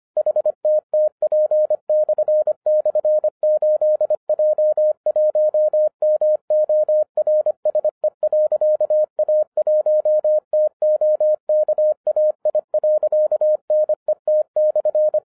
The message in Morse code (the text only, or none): HTTP//8J1MORSE.A1TOKAI.NET/